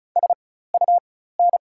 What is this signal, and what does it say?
Morse code, heard as a keyed tone, about 36 wpm, spaced as for 16 wpm, 700 Hz. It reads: SUN
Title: 4x Character Spacing - Top 300 Words - 35wpm